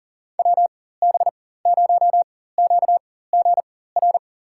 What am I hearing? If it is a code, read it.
Morse code, heard as a keyed tone, about 40 wpm, spaced as for 19 wpm, 700 Hz. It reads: WB0QGR
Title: USA Call Signs - 6 Characters - 4x Character Spacing - 40wpm